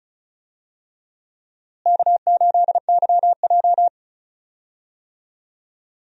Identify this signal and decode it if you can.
Morse code, heard as a keyed tone, about 36 wpm, 700 Hz. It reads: K8YJ
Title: USA Call Signs - 4 Characters - Repeated 3x - 35wpm